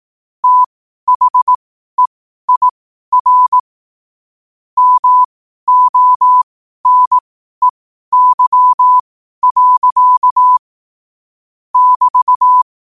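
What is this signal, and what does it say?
Morse code, heard as a keyed tone, about 18 wpm, spaced as for 12 wpm, 1000 Hz. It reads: THEIR MONEY. <BT>
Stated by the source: 12 wpm Morse.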